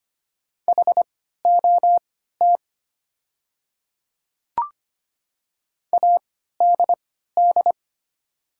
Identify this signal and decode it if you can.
Morse code, heard as a keyed tone, about 25 wpm, spaced as for 14 wpm, 700 Hz. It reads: HOT ADD
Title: 3x Character Spacing - Top 300 Words - 25wpm